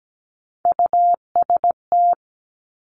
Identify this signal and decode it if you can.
Morse code, heard as a keyed tone, about 17 wpm, 700 Hz. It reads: UST
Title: Three Challenging Letter-Number Combinations 17wpm